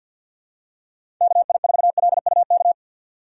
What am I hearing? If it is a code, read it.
Morse code, heard as a keyed tone, about 50 wpm, 700 Hz. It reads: KI4LUK
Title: USA Call Signs - 6 Characters - 50wpm